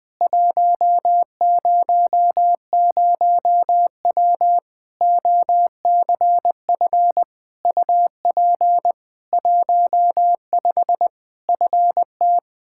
Morse code, heard as a keyed tone, about 20 wpm, 700 Hz. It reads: 100W OCF UP 15 FT
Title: Practice QSO - Exchange 2 - Info - 20wpm